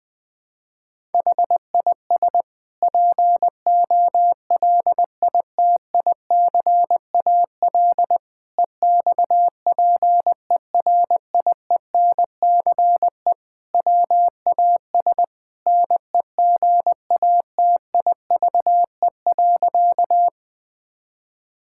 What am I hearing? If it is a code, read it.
Morse code, heard as a keyed tone, about 20 wpm, 700 Hz. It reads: HIS POLITICAL EXPERIENCE WAS NEGATIVE.
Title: Sentences from Top 1000 Words - Review Part 6 - 20wpm